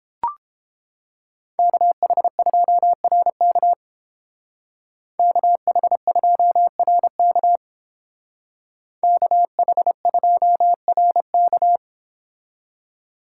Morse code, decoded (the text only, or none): KH2RK KH2RK KH2RK